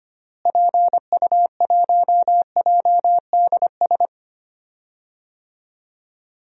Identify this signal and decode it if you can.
Morse code, heard as a keyed tone, about 25 wpm, 700 Hz. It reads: PU1JDS
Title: DXCC-100 Call Signs - 6 Characters - Repeated 2x - 25wpm